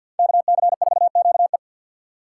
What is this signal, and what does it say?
Morse code, heard as a keyed tone, about 49 wpm, 700 Hz. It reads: KC4QE
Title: USA Call Signs - 3x Character Spacing - 50wpm